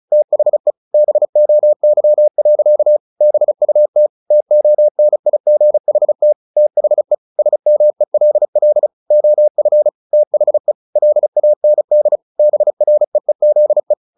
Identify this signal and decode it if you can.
Morse code, heard as a keyed tone, about 34 wpm, 600 Hz. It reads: THE BOY. BUT TONIGHT THE SMELL OF THE LAND BREEZE
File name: the old man and the sea / the old man and the sea 1 35wpm